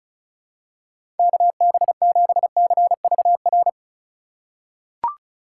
Morse code, heard as a keyed tone, about 36 wpm, 700 Hz. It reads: KB7CVR
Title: USA Call Signs - 6 Characters - 2x Character Spacing - 35wpm